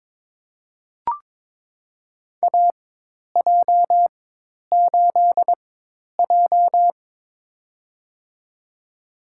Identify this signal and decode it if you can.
Morse code, heard as a keyed tone, about 22 wpm, spaced as for 10 wpm, 700 Hz. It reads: AJ8J
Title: USA Call Signs - 4 Characters - 4x Character Spacing - 22wpm